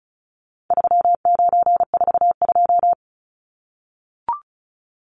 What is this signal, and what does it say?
Morse code, heard as a keyed tone, about 34 wpm, 700 Hz. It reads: 3942
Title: Numbers - 4-Digit 35wpm